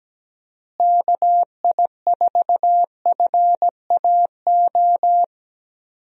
Morse code, heard as a keyed tone, about 17 wpm, 700 Hz. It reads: KI4FAO
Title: USA Call Signs - 6 Characters - Repeated 3x - 17wpm